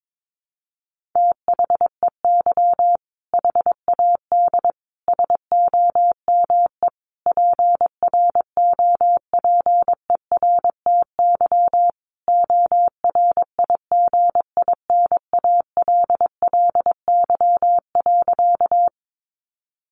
Morse code, heard as a keyed tone, about 22 wpm, 700 Hz. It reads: THEY HAD SOME PROPERTY ORIGINALLY.
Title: Sentences from Top 1500 Words 22wpm